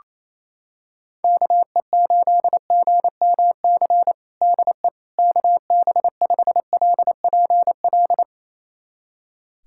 Morse code, heard as a keyed tone, about 27 wpm, 700 Hz. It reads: KE8GMC DE KB5LPL
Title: USA Callsign DE Callsign - 6 Characters - 28wpm